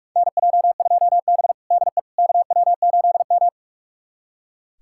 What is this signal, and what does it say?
Morse code, heard as a keyed tone, about 41 wpm, 700 Hz. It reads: NJ2B DE KW8M
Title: USA Callsign DE Callsign - 4 Characters - 45wpm